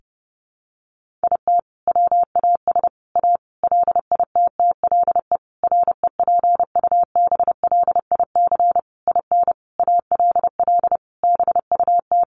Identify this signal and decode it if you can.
Morse code, heard as a keyed tone, about 30 wpm, 700 Hz. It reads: IT WAS A LITTLE REPUBLIC IN ALL BUT
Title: Sentences from Top 1500 Words 30wpm